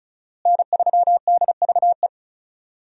Morse code, heard as a keyed tone, about 36 wpm, 700 Hz. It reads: N3DVE
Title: USA Call Signs - 5 Characters - Repeated 3x - 35wpm